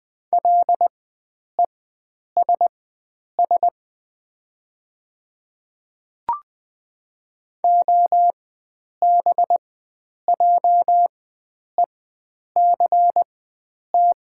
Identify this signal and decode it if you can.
Morse code, heard as a keyed tone, about 20 wpm, spaced as for 9 wpm, 700 Hz. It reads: LESS OBJECT
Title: 4x Character Spacing - Top 500 Words - 20wpm